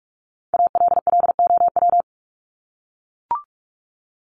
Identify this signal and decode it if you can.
Morse code, heard as a keyed tone, about 44 wpm, 700 Hz. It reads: ALLOW
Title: ED Suffix - Top 100 Words with Base Words 45wpm